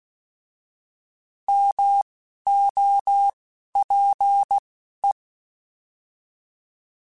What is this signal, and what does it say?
Morse code, heard as a keyed tone, about 16 wpm, 800 Hz. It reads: M O P E